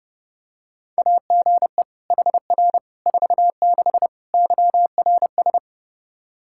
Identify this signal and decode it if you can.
Morse code, heard as a keyed tone, about 30 wpm, 700 Hz. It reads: AGE HR 46 YRS
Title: K1USN SST QSO Practice 30wpm